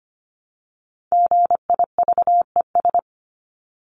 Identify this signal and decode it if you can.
Morse code, heard as a keyed tone, about 25 wpm, 700 Hz. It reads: GIVES
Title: ES:S Suffix - Top 100 Words with Base Words 25wpm